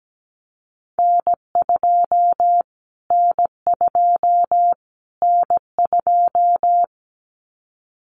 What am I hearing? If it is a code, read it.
Morse code, heard as a keyed tone, about 17 wpm, 700 Hz. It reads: N2 N2 N2
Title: Callsign Prefixes for Top 25 DXCC - Repeated 3x - 17wpm